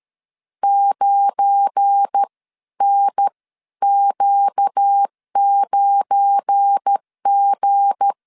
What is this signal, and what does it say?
Morse code, heard as a keyed tone, about 13 wpm, spaced as for 11 wpm, 800 Hz. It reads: 9NQ9G
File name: LEZIONI / 7 GF9Z / 7 GF9 12wpm 2319